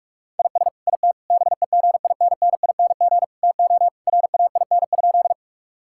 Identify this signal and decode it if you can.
Morse code, heard as a keyed tone, about 46 wpm, 700 Hz. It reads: IS IT BEGINNING TO RAIN?
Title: Sentences from Top 1000 Words - Review Part 5 - 45wpm